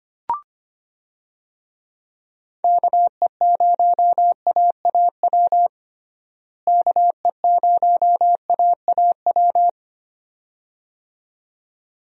Morse code, heard as a keyed tone, about 25 wpm, 700 Hz. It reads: KE0AAW KE0AAW
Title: USA Call Signs - 6 Characters - Repeated 2x - 25wpm